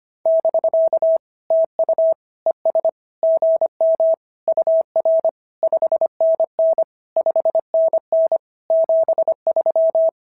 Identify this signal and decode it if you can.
Morse code, heard as a keyed tone, about 22 wpm, 650 Hz. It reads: <BK> TU ES GM UR 5NN 5NN 73
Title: SOTA QSO Practice 25wpm